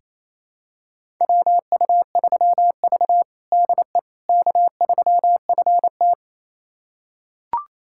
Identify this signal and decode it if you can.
Morse code, heard as a keyed tone, about 28 wpm, 700 Hz. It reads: WU3V DE K3FT
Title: USA Callsign DE Callsign - 4 Characters - 28wpm